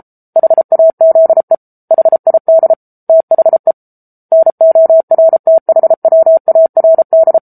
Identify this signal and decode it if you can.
Morse code, heard as a keyed tone, about 34 wpm, 650 Hz. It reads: HAZE HID THE NORTHWARD
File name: Book Two / 07-Chapter 7 - 33 WPM